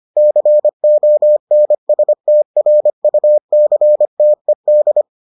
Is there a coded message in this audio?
Yes: Morse code from 0.2 to 5.0 s, heard as a keyed tone, about 25 wpm, 600 Hz. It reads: CONSTRUCTED